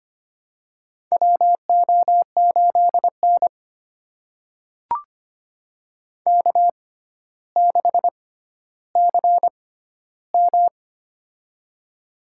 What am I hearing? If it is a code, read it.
Morse code, heard as a keyed tone, about 25 wpm, spaced as for 13 wpm, 700 Hz. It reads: WO8N K6CM
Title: USA Call Signs - 4 Characters - 6x Character Spacing - 25wpm